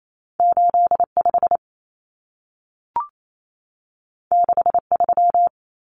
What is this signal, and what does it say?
Morse code, heard as a keyed tone, about 28 wpm, 700 Hz. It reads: 85 63
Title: Common Cut Numbers - 2-Digit 28wpm